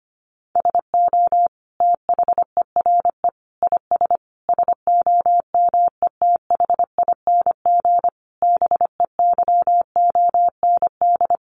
Morse code, heard as a keyed tone, about 25 wpm, 700 Hz. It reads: SO THERE IS SOMETHING BEYOND
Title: Sentences from Top 1500 Words 25wpm